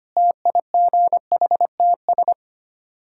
Morse code, heard as a keyed tone, about 25 wpm, 700 Hz. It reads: TIGHTS
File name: practice / 25wpm / Difficult Words with Hints - 25wpm